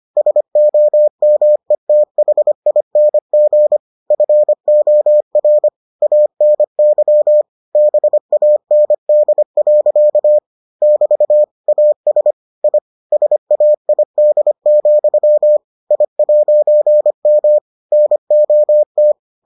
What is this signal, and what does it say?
Morse code, heard as a keyed tone, about 25 wpm, 600 Hz. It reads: SOMETHING FOR ANY BAND. <BT> AS I SAID, I'M NOT